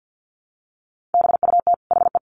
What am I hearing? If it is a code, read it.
Morse code, heard as a keyed tone, about 50 wpm, 700 Hz. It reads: BUT HE
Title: Sets of 2 Words 50wpm